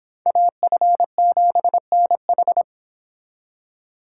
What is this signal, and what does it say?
Morse code, heard as a keyed tone, about 26 wpm, 700 Hz. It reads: AF7NH